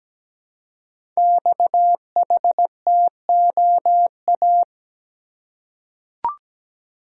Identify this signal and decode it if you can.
Morse code, heard as a keyed tone, about 17 wpm, 700 Hz. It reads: XHTOA
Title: Code Groups - 5 Letters - 17wpm